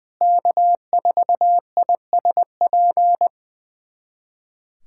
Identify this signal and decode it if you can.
Morse code, heard as a keyed tone, about 20 wpm, 700 Hz. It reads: K4ISP
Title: USA Callsign DE Callsign - 5 Characters - 20wpm